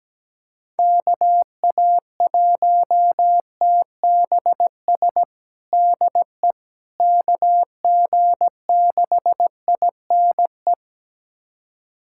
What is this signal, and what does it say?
Morse code, heard as a keyed tone, about 17 wpm, 700 Hz. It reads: KA1TBS DE KG6INE
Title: USA Callsign DE Callsign 17wpm